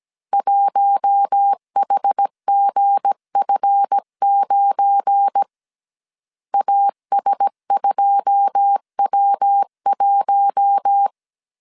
Morse code, heard as a keyed tone, about 17 wpm, 800 Hz. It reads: 1HGF9 AS2W1
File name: CW / DET / 1 9 1500C S A 16wpm 2828